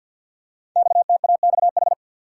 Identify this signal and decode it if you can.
Morse code, heard as a keyed tone, about 48 wpm, 700 Hz. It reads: XTAXH
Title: Code Groups - 5 Letters - 50wpm